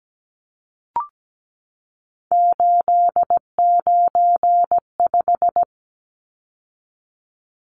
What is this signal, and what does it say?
Morse code, heard as a keyed tone, about 17 wpm, 700 Hz. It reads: 895
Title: Numbers - 3-Digit 17wpm